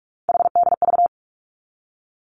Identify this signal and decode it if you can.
Morse code, heard as a keyed tone, about 46 wpm, 700 Hz. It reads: HDV
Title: Three Challenging Letter-Number Combinations 45wpm